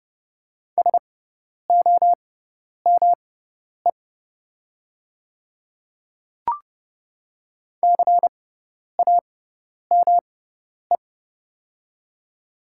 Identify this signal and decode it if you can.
Morse code, heard as a keyed tone, about 30 wpm, spaced as for 10 wpm, 700 Hz. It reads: SOME CAME
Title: 6x Character Spacing - Top 200 Words - 30wpm